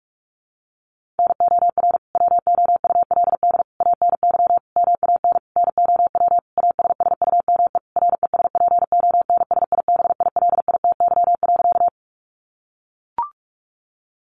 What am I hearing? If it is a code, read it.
Morse code, heard as a keyed tone, about 44 wpm, 700 Hz. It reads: NOR WOULD ANY MAN NOW ASSUME RESPONSIBILITY.
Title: Sentences from Top 1500 Words 45wpm